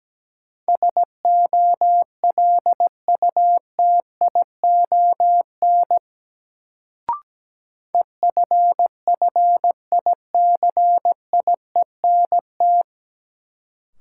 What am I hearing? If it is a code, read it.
Morse code, heard as a keyed tone, about 17 wpm, 700 Hz. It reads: SOLUTION EFFICIENT
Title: Sentences from Top 1300 Words - New Word Drill - 17wpm